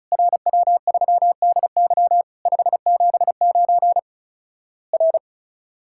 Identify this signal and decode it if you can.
Morse code, heard as a keyed tone, about 36 wpm, 700 Hz. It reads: RW3DY 579 R
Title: Worked All Europe DX Contest - QTC Exchange - 35wpm